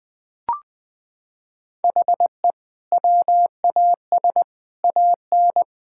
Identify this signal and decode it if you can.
Morse code, heard as a keyed tone, about 21 wpm, 700 Hz. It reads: HE WAS AN
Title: Sentences from Top 1000 Words - Review Part 6 - 20wpm